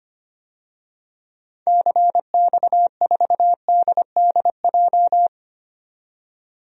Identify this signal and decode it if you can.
Morse code, heard as a keyed tone, about 25 wpm, 700 Hz. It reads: CX4DDJ
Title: DXCC-100 Call Signs - 6 Characters - 25wpm